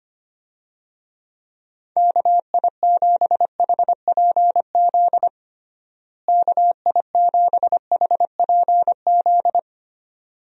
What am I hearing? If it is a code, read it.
Morse code, heard as a keyed tone, about 25 wpm, 700 Hz. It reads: KI7HPZ KI7HPZ
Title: USA Call Signs - 6 Characters - Repeated 2x - 25wpm